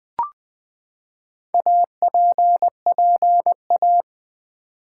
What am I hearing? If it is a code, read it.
Morse code, heard as a keyed tone, about 20 wpm, 700 Hz. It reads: APPA
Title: Build A Word - 4 Letters At A Time - 10-Letter Words - 20wpm